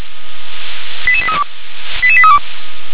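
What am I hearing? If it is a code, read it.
Five-tone selective call (ZVEI), heard as quick runs of five tones, tones of about 70 ms. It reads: 88832 88832
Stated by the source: Five-tone ZVEI selective call received on an SDR.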